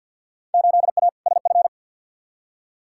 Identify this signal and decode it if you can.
Morse code, heard as a keyed tone, about 50 wpm, 700 Hz. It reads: 8A SF
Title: Field Day Exchanges Encore - 50wpm